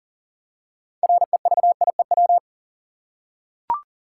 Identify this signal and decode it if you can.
Morse code, heard as a keyed tone, about 38 wpm, 700 Hz. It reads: REVIEW
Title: Six Letter Words - 40wpm